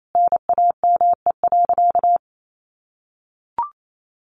Morse code, heard as a keyed tone, about 28 wpm, 700 Hz. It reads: NAME.
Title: Sentences from Top 1500 Words 28wpm